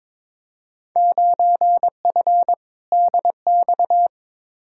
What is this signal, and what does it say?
Morse code, heard as a keyed tone, about 21 wpm, 700 Hz. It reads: 9F DX
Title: Field Day Exchanges Encore - 22wpm